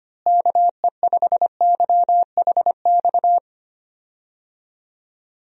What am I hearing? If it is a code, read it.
Morse code, heard as a keyed tone, about 25 wpm, 700 Hz. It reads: KE5YHX
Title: USA Call Signs - 6 Characters - Repeated 3x - 25wpm